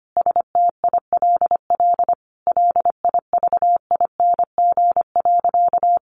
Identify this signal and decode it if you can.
Morse code, heard as a keyed tone, about 25 wpm, 700 Hz. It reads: STILL LIVING.
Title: Sentences from Top 1100 Words 25wpm